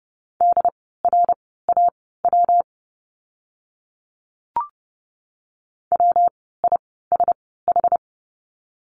Morse code, heard as a keyed tone, about 30 wpm, spaced as for 17 wpm, 700 Hz. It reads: DRAW WISH
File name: course / 30:10wpm / 115 Lesson 19f - Letter w - Words 30wpm